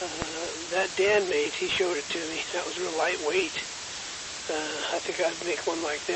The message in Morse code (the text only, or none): none